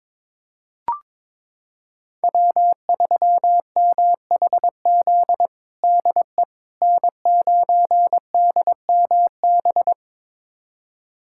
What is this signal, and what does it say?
Morse code, heard as a keyed tone, about 22 wpm, 700 Hz. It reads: W3MHZ DE N9DMB
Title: USA Callsign DE Callsign - 5 Characters - 22wpm